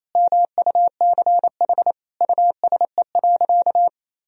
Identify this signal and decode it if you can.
Morse code, heard as a keyed tone, about 27 wpm, 700 Hz. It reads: MUCH USE.
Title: Sentences from Top 1000 Words - Review Part 3 - 28wpm